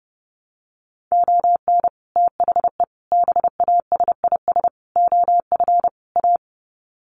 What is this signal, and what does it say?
Morse code, heard as a keyed tone, about 30 wpm, 700 Hz. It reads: ON THE BASIS OF A
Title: Sets of 5 Words Encore 30wpm